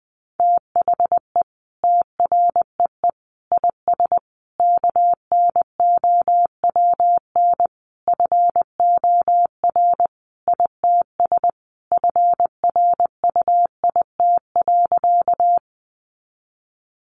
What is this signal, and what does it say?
Morse code, heard as a keyed tone, about 20 wpm, 700 Hz. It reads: THE TREE IS KNOWN FOR ITS FRUIT.
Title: Sentences from Top 1300 Words 20wpm